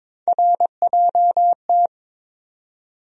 Morse code, heard as a keyed tone, about 22 wpm, 700 Hz. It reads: RJT
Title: USA Call Sign Postfixes - 3 Letters 22wpm